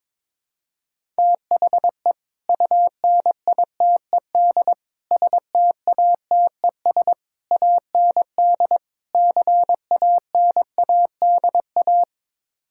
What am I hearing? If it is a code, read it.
Morse code, heard as a keyed tone, about 22 wpm, 700 Hz. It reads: THE UNITED STATES AND CANADA
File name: practice / 22wpm / Build A Phrase - 5-Word Phrases - 22wpm